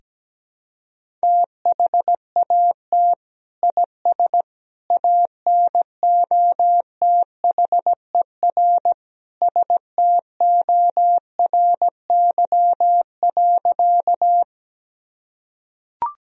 Morse code, heard as a keyed tone, about 17 wpm, 700 Hz. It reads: THAT IS ANOTHER STORY.
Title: Sentences from Top 1000 Words - Review Part 1 - 17wpm